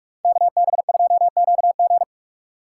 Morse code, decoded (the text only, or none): KB2QG